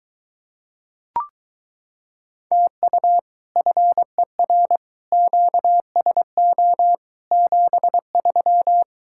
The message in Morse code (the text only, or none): TU FER QSO 73